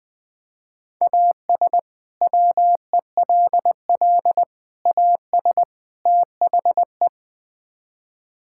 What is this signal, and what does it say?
Morse code, heard as a keyed tone, about 20 wpm, 700 Hz. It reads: AS WELL AS THE
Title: Build A Phrase - 4-Word Phrases - 20wpm